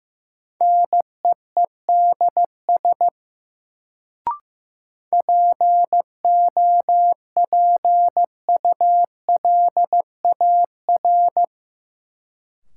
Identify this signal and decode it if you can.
Morse code, heard as a keyed tone, about 15 wpm, 700 Hz. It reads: NEEDS POPULAR
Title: Sentences from Top 1000 Words - Review Part 3 - New Word Drill - 15wpm